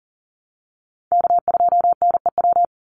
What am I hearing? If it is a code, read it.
Morse code, heard as a keyed tone, about 40 wpm, 700 Hz. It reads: K2NEW
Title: USA Call Signs 40wpm